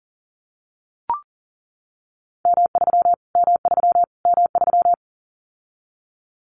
Morse code, heard as a keyed tone, about 40 wpm, 700 Hz. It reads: M3 M3 M3
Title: Callsign Prefixes for Top 25 DXCC - Repeated 3x - 40wpm